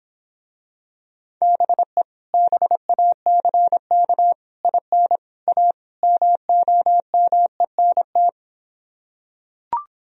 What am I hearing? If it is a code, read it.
Morse code, heard as a keyed tone, about 26 wpm, 700 Hz. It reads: BE BACK IN A MOMENT